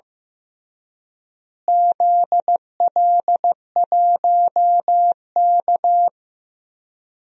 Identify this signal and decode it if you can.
Morse code, heard as a keyed tone, about 15 wpm, 700 Hz. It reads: ZL1K
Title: DXCC-100 Call Signs - 4 Characters - Repeated 2x - 15wpm